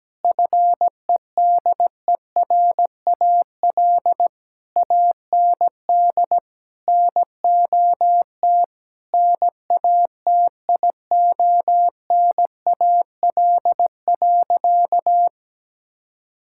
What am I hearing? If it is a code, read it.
Morse code, heard as a keyed tone, about 17 wpm, 700 Hz. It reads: FEDERAL AND NOT NATIONAL.
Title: Sentences from Top 1000 Words - Review Part 5 - 17wpm